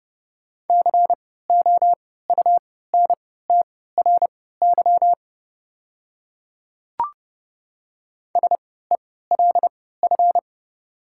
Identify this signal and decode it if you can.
Morse code, heard as a keyed tone, about 32 wpm, spaced as for 17 wpm, 700 Hz. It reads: COUNTRY SELF
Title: 3x Character Spacing - Top 300 Words - 30wpm